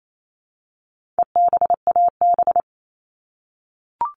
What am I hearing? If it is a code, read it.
Morse code, heard as a keyed tone, about 28 wpm, 700 Hz. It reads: EBAB
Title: All Cut Numbers - 4-Digit 28wpm